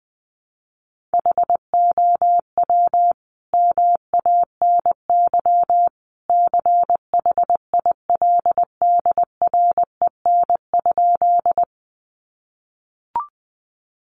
Morse code, heard as a keyed tone, about 20 wpm, 700 Hz. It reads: HOW MANY CHILDREN?
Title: Sentences from Top 300 Words 20wpm